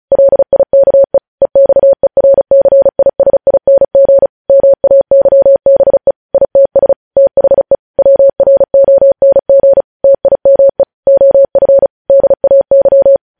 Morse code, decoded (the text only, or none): LIKE EXERCISING MAYBE ITS THE WRONG TIME OF DAY